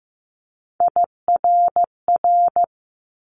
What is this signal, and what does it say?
Morse code, heard as a keyed tone, about 15 wpm, 700 Hz. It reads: IRR
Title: Multi-letter Phonemes Quick Fire 15wpm